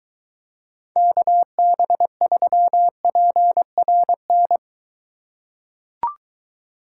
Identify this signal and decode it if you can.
Morse code, heard as a keyed tone, about 23 wpm, 700 Hz. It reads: KB3PRN